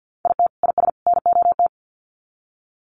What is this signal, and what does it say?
Morse code, heard as a keyed tone, about 50 wpm, 700 Hz. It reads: IT IS NOT
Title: Sets of 3 Words 50wpm